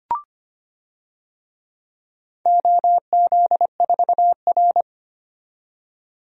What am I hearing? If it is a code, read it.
Morse code, heard as a keyed tone, about 25 wpm, 700 Hz. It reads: OZ4R
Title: DXCC-100 Call Signs - 4 Characters - 25wpm